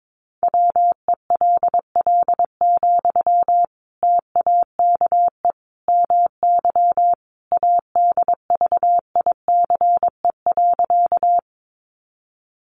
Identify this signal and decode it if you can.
Morse code, heard as a keyed tone, about 22 wpm, 700 Hz. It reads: WELL, TAKE MY ADVICE.
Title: Sentences from Top 1100 Words 22wpm